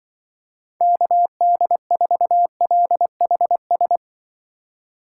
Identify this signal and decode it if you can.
Morse code, heard as a keyed tone, about 24 wpm, 700 Hz. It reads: KD4LHS
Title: Speed Racing - USA 2x3 Call Signs 20wpm